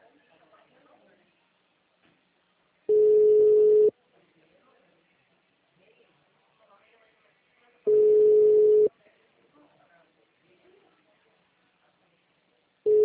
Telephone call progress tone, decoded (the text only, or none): ringback tone